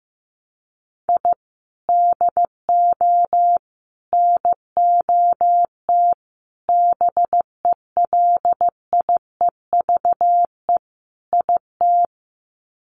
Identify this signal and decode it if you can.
Morse code, heard as a keyed tone, about 15 wpm, 700 Hz. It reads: I DO NOT BELIEVE IT
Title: Sets of 5 Words 15wpm